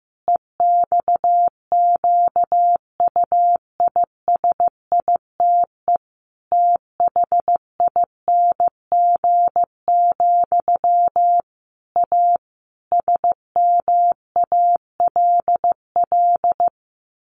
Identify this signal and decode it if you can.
Morse code, heard as a keyed tone, about 15 wpm, 700 Hz. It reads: EXQUISITE THING, A SMALL
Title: A Sound Of Thunder 15wpm